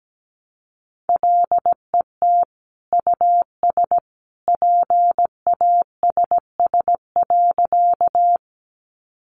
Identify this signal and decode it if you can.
Morse code, heard as a keyed tone, about 17 wpm, 700 Hz. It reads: LET US PASS.
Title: Sentences from Top 400 Words 17wpm